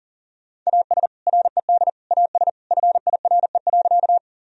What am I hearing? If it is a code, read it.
Morse code, heard as a keyed tone, about 40 wpm, 700 Hz. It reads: AS RED AS FIRE.
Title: Sentences from Top 1000 Words - Review Part 2 - 40wpm